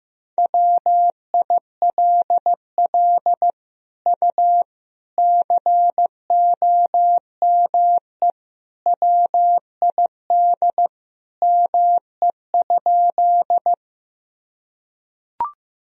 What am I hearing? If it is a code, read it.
Morse code, heard as a keyed tone, about 15 wpm, 700 Hz. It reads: WILL U COME WID ME?
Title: Sentences from Top 200 Words and Abbreviations 15wpm